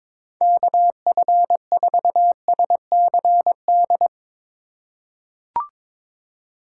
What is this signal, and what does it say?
Morse code, heard as a keyed tone, about 21 wpm, 700 Hz. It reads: KF4SCD
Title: USA Call Signs - 4x Character Spacing - 22wpm